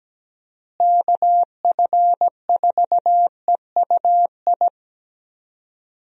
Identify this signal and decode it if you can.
Morse code, heard as a keyed tone, about 17 wpm, 700 Hz. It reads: KF4EUI
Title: USA Call Signs - 6 Characters - Repeated 2x - 17wpm